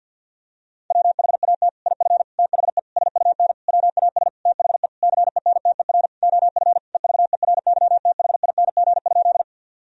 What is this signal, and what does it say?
Morse code, heard as a keyed tone, about 51 wpm, 700 Hz. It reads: WHAT IF THE SUN WAS THE CENTER OF EVERYTHING?
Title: Sentences from Top 1000 Words - Review Part 2 - 50wpm